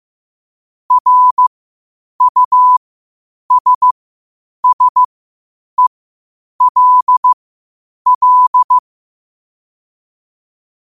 Morse code, heard as a keyed tone, about 15 wpm, spaced as for 8 wpm, 1000 Hz. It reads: RUSSELL
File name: qotdfiles / 8wpm